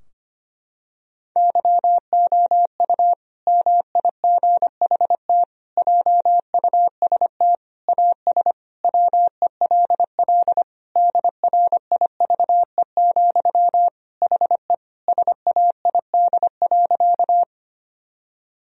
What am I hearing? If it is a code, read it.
Morse code, heard as a keyed tone, about 25 wpm, 700 Hz. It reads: YOU MIGHT JUST AS WELL DRIVE, HE SAID.
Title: Sentences from Top 1000 Words - Review Part 2 - 25wpm